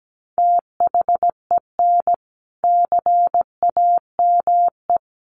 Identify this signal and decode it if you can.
Morse code, heard as a keyed tone, about 17 wpm, 700 Hz. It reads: THEN CAME
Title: Sentences from Top 1200 Words 17wpm